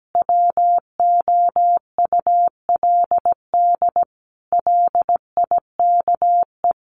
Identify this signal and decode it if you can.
Morse code, heard as a keyed tone, about 17 wpm, 700 Hz. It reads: WOULD LIKE
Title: Sentences from Top 500 Words 17wpm